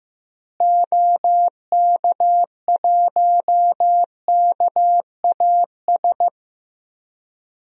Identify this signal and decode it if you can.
Morse code, heard as a keyed tone, about 15 wpm, 700 Hz. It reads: OK1KAS
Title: DXCC-100 Call Signs - 6 Characters - Repeated 3x - 15wpm